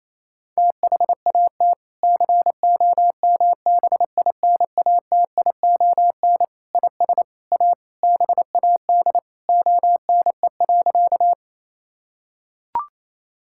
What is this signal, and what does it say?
Morse code, heard as a keyed tone, about 28 wpm, 700 Hz. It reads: THAT COMBINATION IS A BAD ONE.
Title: Sentences from Top 1000 Words - Review Part 6 - 28wpm